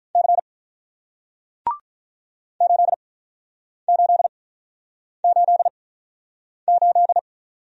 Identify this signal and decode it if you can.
Morse code, heard as a keyed tone, about 48 wpm, spaced as for 9 wpm, 700 Hz. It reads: C 8888